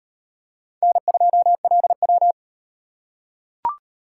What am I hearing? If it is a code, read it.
Morse code, heard as a keyed tone, about 38 wpm, 700 Hz. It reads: N2LW